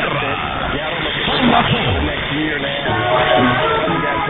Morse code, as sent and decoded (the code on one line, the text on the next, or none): none